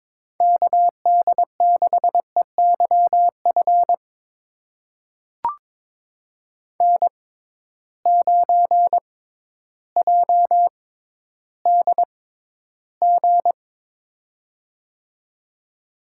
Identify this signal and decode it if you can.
Morse code, heard as a keyed tone, about 22 wpm, spaced as for 12 wpm, 700 Hz. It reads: KD6EYF N9JDG
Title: USA Call Signs - 6x Character Spacing - 22wpm